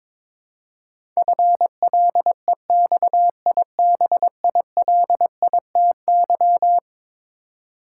Morse code, heard as a keyed tone, about 22 wpm, 700 Hz. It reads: FLEXIBILITY
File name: practice / 22wpm / Eleven Letter Words - 22wpm